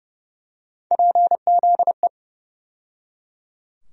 Morse code, heard as a keyed tone, about 30 wpm, 700 Hz. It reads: PZE
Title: USA Call Sign Postfixes - 3 Letters 30wpm